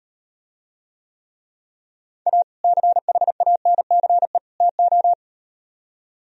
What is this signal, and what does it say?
Morse code, heard as a keyed tone, about 38 wpm, 700 Hz. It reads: A CHANCE TO